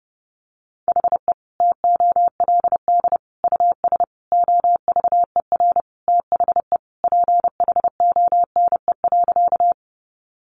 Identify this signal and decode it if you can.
Morse code, heard as a keyed tone, about 30 wpm, 700 Hz. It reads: HE TOLD US OVER THE PHONE.
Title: Sentences from Top 1100 Words 30wpm